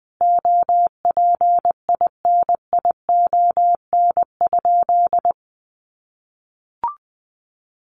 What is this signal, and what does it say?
Morse code, heard as a keyed tone, about 20 wpm, 700 Hz. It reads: OPINION?
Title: Sentences from Top 1200 Words 20wpm